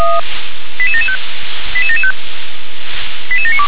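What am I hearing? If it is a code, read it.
Five-tone selective call (ZVEI), heard as quick runs of five tones, tones of about 70 ms. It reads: 88775 88775 88792
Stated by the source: ZVEI five-tone selective call 88465, received with an SDR.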